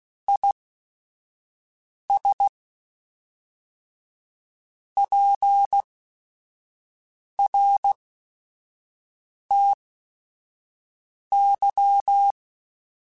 Morse code, heard as a keyed tone, about 16 wpm, spaced as for 5 wpm, 800 Hz. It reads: IS PRTY
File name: QSOs / QSOs-mp3s / WA7QOX 1@5